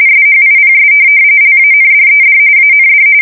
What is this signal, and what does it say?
RTTY, about 45.45 baud. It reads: SHIPS LANDED AND
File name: fsk170 / USS-New-Jersey